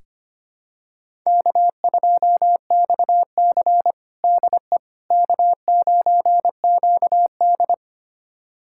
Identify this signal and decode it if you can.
Morse code, heard as a keyed tone, about 25 wpm, 700 Hz. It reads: K2XC DE K9QD
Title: USA Callsign DE Callsign - 4 Characters - 25wpm